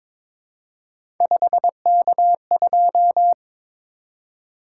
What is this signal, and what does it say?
Morse code, heard as a keyed tone, about 22 wpm, 700 Hz. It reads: 5K2
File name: practice / 22wpm / Three Letter-Number Combinations - 22wpm